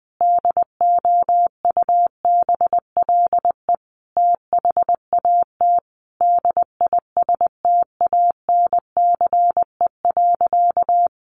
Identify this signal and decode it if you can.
Morse code, heard as a keyed tone, about 20 wpm, 700 Hz. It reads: DOUBLE THAT DISTANCE.
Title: Sentences from Top 1400 Words 20wpm